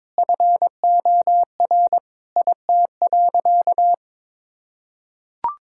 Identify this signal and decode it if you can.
Morse code, heard as a keyed tone, about 22 wpm, 700 Hz. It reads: FOR IT.
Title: Sentences from Top 1600 Words 22wpm